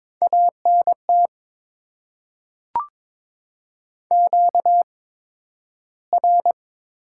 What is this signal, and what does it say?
Morse code, heard as a keyed tone, about 22 wpm, spaced as for 12 wpm, 700 Hz. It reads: ANT QR
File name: Top 20 QSO Elements - 8x Character Spacing - 22wpm